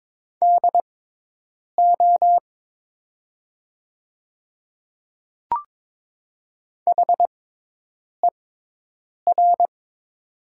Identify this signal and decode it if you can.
Morse code, heard as a keyed tone, about 22 wpm, spaced as for 8 wpm, 700 Hz. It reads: DO HER
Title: 6x Character Spacing - Top 100 Words - 22wpm